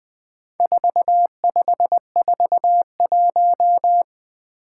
Morse code, heard as a keyed tone, about 21 wpm, 700 Hz. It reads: 4541
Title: Four Challenging Letter-Number Combinations 20wpm